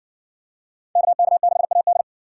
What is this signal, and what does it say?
Morse code, heard as a keyed tone, about 60 wpm, 700 Hz. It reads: KK6AD